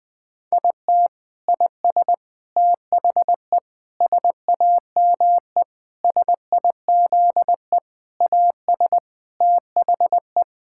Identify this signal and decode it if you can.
Morse code, heard as a keyed tone, about 21 wpm, 700 Hz. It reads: IT IS THE SAME SIZE AS THE
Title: Sentences from Top 1000 Words - Review Part 6 - 20wpm